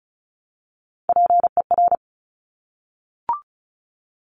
Morse code, heard as a keyed tone, about 36 wpm, 700 Hz. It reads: PER
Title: Top Three Letter Combinations in Words 35wpm